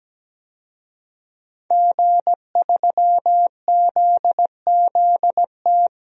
Morse code, heard as a keyed tone, about 17 wpm, 700 Hz. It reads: G3ZZT
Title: DXCC-100 Call Signs - 5 Characters - 17wpm